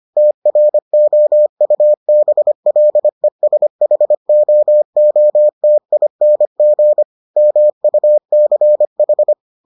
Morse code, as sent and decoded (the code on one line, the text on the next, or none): - .-. --- ..- -... .-.. ... .... --- --- - .. -. --. / -- ..- -.-. ....
TROUBLESHOOTING MUCH